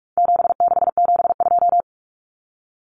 Morse code, heard as a keyed tone, about 44 wpm, 700 Hz. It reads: 7672
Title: Common Cut Numbers - 4-Digit 45wpm